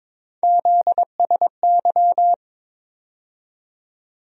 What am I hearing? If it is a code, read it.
Morse code, heard as a keyed tone, about 22 wpm, 700 Hz. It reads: ZSY
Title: USA Call Sign Postfixes - 3 Letters 22wpm